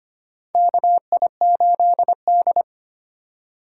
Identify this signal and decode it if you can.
Morse code, heard as a keyed tone, about 25 wpm, 700 Hz. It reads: KI8D
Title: USA Call Signs - 4 Characters - 6x Character Spacing - 25wpm